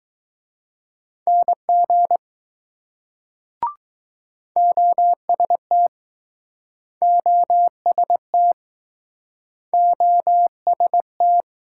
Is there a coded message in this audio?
Yes: Morse code from 1.3 to 11.4 s, heard as a keyed tone, about 20 wpm, 700 Hz. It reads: NG OST OST OST